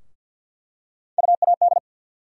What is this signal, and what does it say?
Morse code, heard as a keyed tone, about 53 wpm, 700 Hz. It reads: UAD
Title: USA Call Sign Postfixes - 3 Letters 50wpm